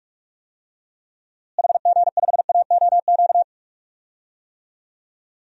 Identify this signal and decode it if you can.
Morse code, heard as a keyed tone, about 46 wpm, 700 Hz. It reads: HG5AOQ
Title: DXCC-100 Call Signs - 6 Characters - Repeated 3x - 45wpm